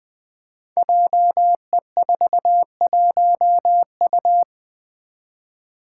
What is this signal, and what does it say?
Morse code, heard as a keyed tone, about 20 wpm, 700 Hz. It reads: JE41U
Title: Five Challenging Letter-Number Combinations 20wpm